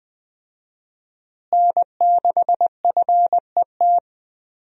Morse code, heard as a keyed tone, about 21 wpm, 700 Hz. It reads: N6FET
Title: USA Call Signs - 5 Characters - Repeated 2x - 20wpm